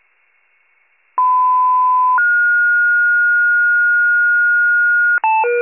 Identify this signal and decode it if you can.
Two-tone page, heard as a one-second tone, then a three-second tone, about 1010 then 1430 Hz. It is heard from 1.2 to 5.2 s.
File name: wcalert